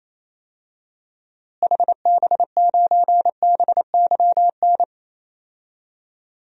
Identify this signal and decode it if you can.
Morse code, heard as a keyed tone, about 28 wpm, 700 Hz. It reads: HB9BYN
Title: DXCC-100 Call Signs - 6 Characters - Repeated 2x - 28wpm